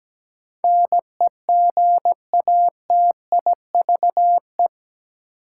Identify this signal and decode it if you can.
Morse code, heard as a keyed tone, about 17 wpm, 700 Hz. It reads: NEGATIVE
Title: Sentences from Top 1000 Words - New Word Drill - 17wpm